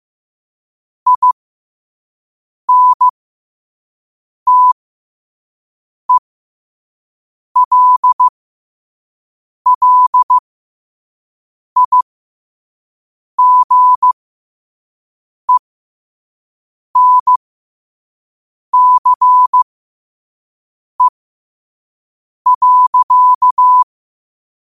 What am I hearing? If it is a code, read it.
Morse code, heard as a keyed tone, about 15 wpm, spaced as for 5 wpm, 1000 Hz. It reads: INTELLIGENCE.
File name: qotdfiles / 5wpm